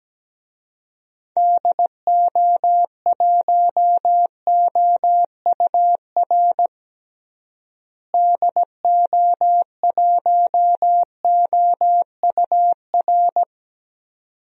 Morse code, heard as a keyed tone, about 17 wpm, 700 Hz. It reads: DO1OUR DO1OUR
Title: DXCC-100 Call Signs - 6 Characters - Repeated 2x - 17wpm